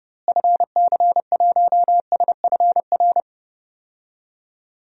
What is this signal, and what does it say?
Morse code, heard as a keyed tone, about 30 wpm, 700 Hz. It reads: FC1SFR
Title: DXCC-100 Call Signs - 6 Characters - Repeated 3x - 30wpm